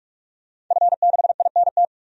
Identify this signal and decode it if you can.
Morse code, heard as a keyed tone, about 46 wpm, 700 Hz. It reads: F6INT
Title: DXCC-100 Call Signs - 5 Characters - 45wpm